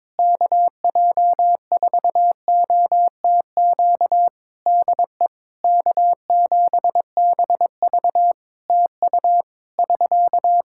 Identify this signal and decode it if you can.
Morse code, heard as a keyed tone, about 22 wpm, 700 Hz. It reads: KJ4OTQ DE K7BV TU <SK>
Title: K1USN SST QSO Practice 22wpm